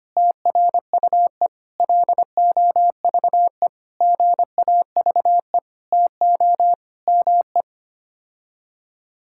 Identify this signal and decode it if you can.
Morse code, heard as a keyed tone, about 25 wpm, 700 Hz. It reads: TRUE LOVE GAVE TO ME
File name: Sets of 9 Words - 25wpm